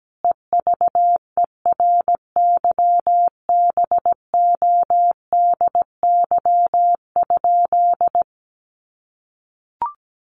Morse code, heard as a keyed tone, about 17 wpm, 700 Hz. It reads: EVERYBODY?
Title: Sentences from Top 1400 Words 17wpm